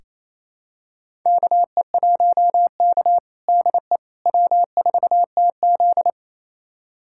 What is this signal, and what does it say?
Morse code, heard as a keyed tone, about 28 wpm, 700 Hz. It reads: KE1K DE W4TZ
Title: USA Callsign DE Callsign - 4 Characters - 28wpm